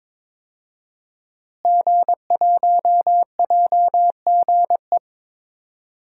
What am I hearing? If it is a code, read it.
Morse code, heard as a keyed tone, about 22 wpm, 700 Hz. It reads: G1JGE
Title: DXCC-100 Call Signs - 5 Characters - Repeated 3x - 22wpm